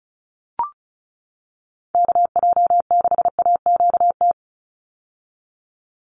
Morse code, heard as a keyed tone, about 34 wpm, 700 Hz. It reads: KJ6AQT
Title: USA Call Signs 35wpm